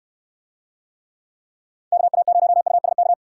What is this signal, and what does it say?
Morse code, heard as a keyed tone, about 69 wpm, 700 Hz. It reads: KA0USK